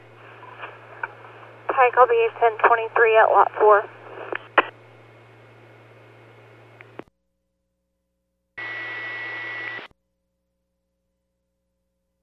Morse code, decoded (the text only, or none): none